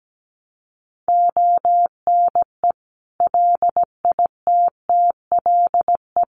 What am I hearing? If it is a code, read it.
Morse code, heard as a keyed tone, about 17 wpm, 700 Hz. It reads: ONE LITTLE
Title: Sentences from Top 1000 Words 17wpm